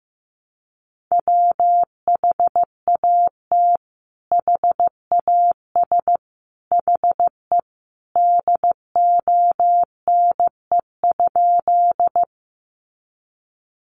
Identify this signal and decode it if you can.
Morse code, heard as a keyed tone, about 15 wpm, 700 Hz. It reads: WHAT HAS HE DONE?
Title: Sentences from Top 500 Words 15wpm